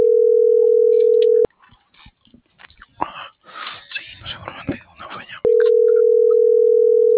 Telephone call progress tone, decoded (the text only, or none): ringback tone